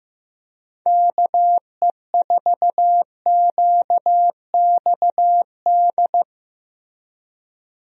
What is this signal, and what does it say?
Morse code, heard as a keyed tone, about 15 wpm, 700 Hz. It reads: KE4QXD
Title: USA Call Signs - 6 Characters - Repeated 3x - 15wpm